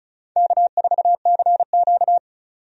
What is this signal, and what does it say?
Morse code, heard as a keyed tone, about 36 wpm, 700 Hz. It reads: K4CQ
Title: USA Call Signs - 4 Characters - 35wpm